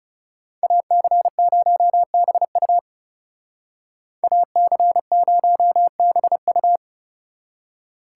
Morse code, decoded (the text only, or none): AC0BU AC0BU